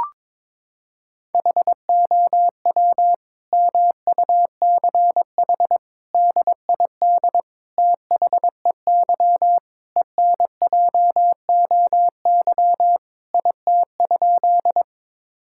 Morse code, heard as a keyed tone, about 22 wpm, 700 Hz. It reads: HOW MUCH DID THEY ENJOY IT?
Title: Sentences from Top 1000 Words - Review Part 5 - 22wpm